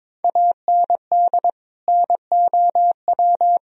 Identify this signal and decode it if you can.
Morse code, heard as a keyed tone, about 21 wpm, 700 Hz. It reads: AND NOW
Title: Sentences from Top 1000 Words - Review Part 2 - 22wpm